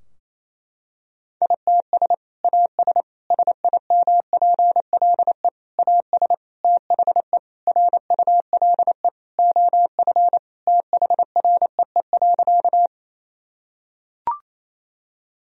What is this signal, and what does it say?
Morse code, heard as a keyed tone, about 28 wpm, 700 Hz. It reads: ITS AS SIMPLE AS THE RULE OF THREE.
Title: Sentences from Top 1000 Words - Review Part 2 - 28wpm